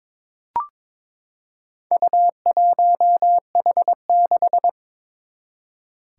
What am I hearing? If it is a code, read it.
Morse code, heard as a keyed tone, about 22 wpm, 700 Hz. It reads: U1H6
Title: Four Challenging Letter-Number Combinations 22wpm